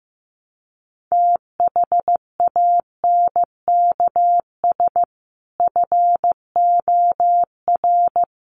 Morse code, heard as a keyed tone, about 15 wpm, 700 Hz. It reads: THANKS FOR
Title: Sentences from Top 1500 Words 15wpm